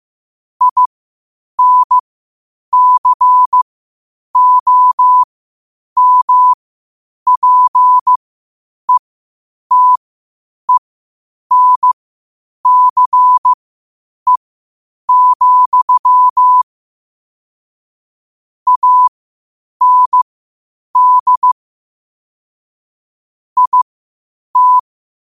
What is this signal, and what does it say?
Morse code, heard as a keyed tone, about 15 wpm, spaced as for 8 wpm, 1000 Hz. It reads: INCOMPETENCE, AND IT
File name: qotdfiles / 8wpm